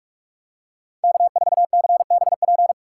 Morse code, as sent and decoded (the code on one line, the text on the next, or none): -.- ....- -.-. -... .--.
K4CBP